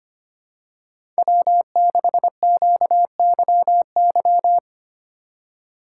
Morse code, heard as a keyed tone, about 25 wpm, 700 Hz. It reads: W6QYY
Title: USA Call Signs - 5 Characters - 3x Character Spacing - 25wpm